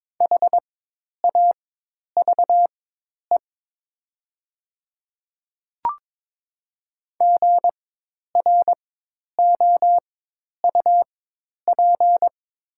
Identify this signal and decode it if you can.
Morse code, heard as a keyed tone, about 22 wpm, spaced as for 10 wpm, 700 Hz. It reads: HAVE GROUP
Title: 4x Character Spacing - Top 300 Words - 22wpm